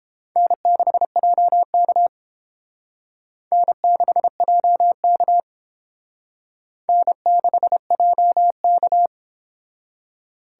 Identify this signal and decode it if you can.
Morse code, heard as a keyed tone, about 31 wpm, 700 Hz. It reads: N6JK N6JK N6JK